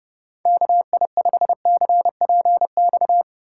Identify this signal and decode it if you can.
Morse code, heard as a keyed tone, about 31 wpm, 700 Hz. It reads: KI5CPX
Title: Speed Racing - USA Call Signs 25wpm